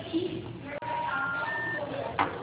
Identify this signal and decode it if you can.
Special information tone, with noise.
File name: out- -2005- - - .61831